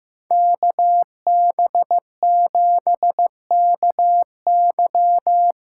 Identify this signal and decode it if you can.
Morse code, heard as a keyed tone, about 15 wpm, 700 Hz. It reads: KB7KY
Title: USA Call Signs - 6 Characters - 8x Character Spacing - 15wpm